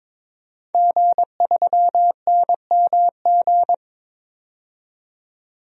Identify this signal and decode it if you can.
Morse code, heard as a keyed tone, about 22 wpm, 700 Hz. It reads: G3NMG
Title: DXCC-100 Call Signs - 5 Characters - 22wpm